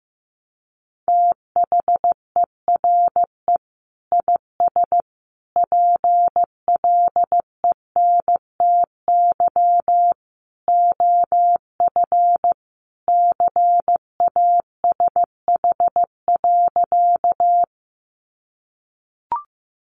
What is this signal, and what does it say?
Morse code, heard as a keyed tone, about 15 wpm, 700 Hz. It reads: THERE IS PLENTY OF CASH.
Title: Sentences from Top 1300 Words 15wpm